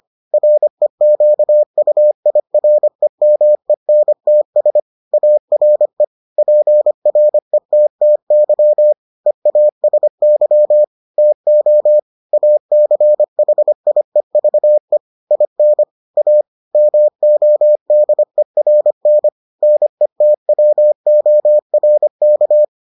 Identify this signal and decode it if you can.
Morse code, heard as a keyed tone, about 25 wpm, 600 Hz. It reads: REQUIREMENTS ARE PRETTY EASY TO ACHIEVE IN A MODERN NETWORK